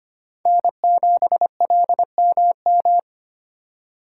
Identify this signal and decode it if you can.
Morse code, heard as a keyed tone, about 32 wpm, 700 Hz. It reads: N7LMM